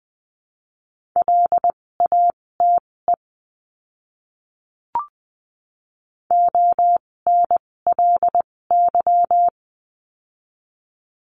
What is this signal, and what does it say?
Morse code, heard as a keyed tone, about 20 wpm, spaced as for 16 wpm, 700 Hz. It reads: LATE ONLY
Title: 256 Lesson 42a - Words 12-20wpm